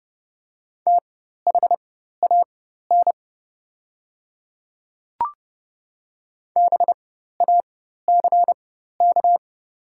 Morse code, heard as a keyed tone, about 30 wpm, spaced as for 14 wpm, 700 Hz. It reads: THAN BACK
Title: 4x Character Spacing - Top 200 Words - 30wpm